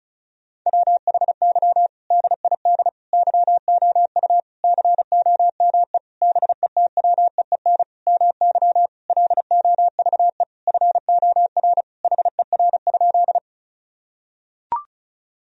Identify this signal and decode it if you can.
Morse code, heard as a keyed tone, about 36 wpm, 700 Hz. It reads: WHY DID YOU COME BETWEEN MY LOVE FOR HER?
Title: Sentences from Top 1000 Words - Review Part 1 - 35wpm